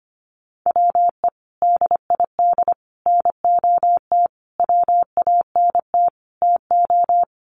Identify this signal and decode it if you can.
Morse code, heard as a keyed tone, about 25 wpm, 700 Hz. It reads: WE DID NOT WANT TO
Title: Sets of 5 Words Encore 25wpm